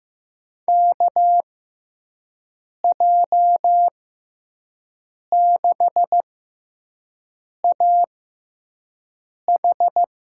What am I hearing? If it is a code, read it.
Morse code, heard as a keyed tone, about 15 wpm, spaced as for 5 wpm, 700 Hz. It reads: KJ6AH
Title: USA Call Signs - 6 Characters - 6x Character Spacing - 15wpm